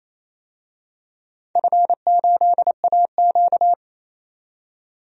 Morse code, heard as a keyed tone, about 27 wpm, 700 Hz. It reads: F8AQ
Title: DXCC-100 Call Signs - 4 Characters - 28wpm